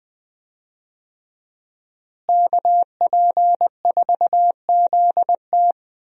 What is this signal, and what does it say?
Morse code, heard as a keyed tone, about 20 wpm, 700 Hz. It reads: KP4ZT
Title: USA Call Signs - 5 Characters - Repeated 2x - 20wpm